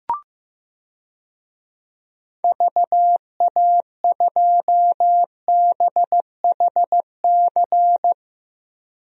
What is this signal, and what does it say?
Morse code, heard as a keyed tone, about 16 wpm, 700 Hz. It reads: VA2BHC
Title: DXCC-100 Call Signs - 6 Characters - Repeated 2x - 15wpm